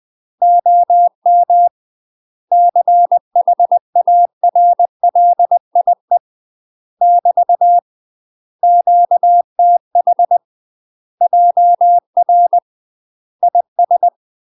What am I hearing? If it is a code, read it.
Morse code, heard as a keyed tone, about 20 wpm, 700 Hz. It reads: OM CHARLIE <BT> QTH JR IS
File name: UK MASTER QSOs TRAINING / 20WPM / UK MASTER QSOs 0023